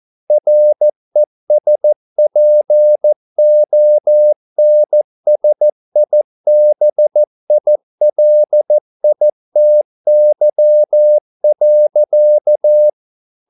Morse code, heard as a keyed tone, about 14 wpm, 600 Hz. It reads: RESPONSIBILITY.